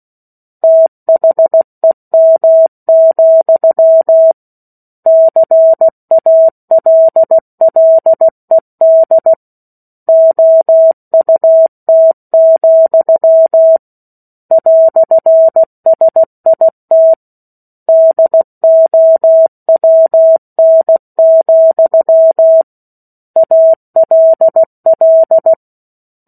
Morse code, heard as a keyed tone, about 16 wpm, 650 Hz. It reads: THEM, CALLED OUT, "SIT DOWN, ALL
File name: 16 WPM